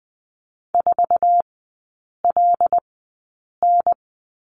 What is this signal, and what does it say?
Morse code, heard as a keyed tone, about 20 wpm, spaced as for 8 wpm, 700 Hz. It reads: 4LN
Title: 184 Lesson 30e - Letter b - Three Characters 20wpm